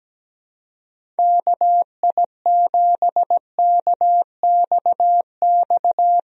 Morse code, heard as a keyed tone, about 17 wpm, 700 Hz. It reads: KI7KXX
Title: USA Call Signs - 6 Characters - Repeated 3x - 17wpm